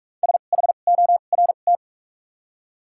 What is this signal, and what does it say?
Morse code, heard as a keyed tone, about 46 wpm, spaced as for 33 wpm, 700 Hz. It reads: SHORT
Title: 2x Character Spacing - Top 500 Words - 45wpm